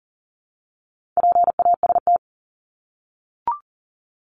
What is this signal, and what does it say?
Morse code, heard as a keyed tone, about 40 wpm, 700 Hz. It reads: PAST
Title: Top 501-750 Words 40wpm